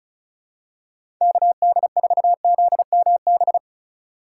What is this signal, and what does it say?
Morse code, heard as a keyed tone, about 34 wpm, 700 Hz. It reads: KD4ZMB
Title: USA Call Signs - 6 Characters - 2x Character Spacing - 35wpm